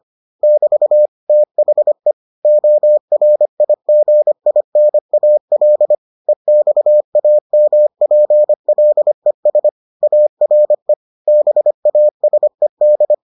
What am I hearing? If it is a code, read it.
Morse code, heard as a keyed tone, about 25 wpm, 600 Hz. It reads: <BT> THE ORIGINAL EXAMPLES ARE BASED